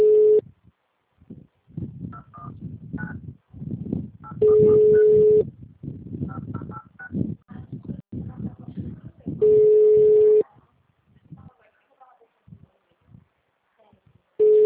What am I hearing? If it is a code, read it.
Ringback tone (Europe, 425 Hz), with speech.